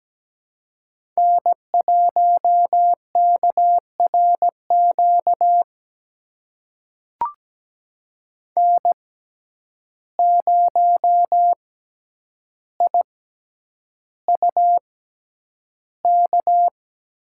Morse code, heard as a keyed tone, about 17 wpm, spaced as for 9 wpm, 700 Hz. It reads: N1KRQ N0IUK
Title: USA Call Signs - 5 Characters - 6x Character Spacing - 17wpm